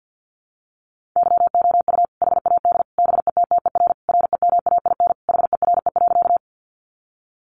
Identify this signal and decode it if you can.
Morse code, heard as a keyed tone, about 50 wpm, 700 Hz. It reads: YOU HAD BETTER REMAIN HERE.
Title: Sentences from Top 1200 Words 50wpm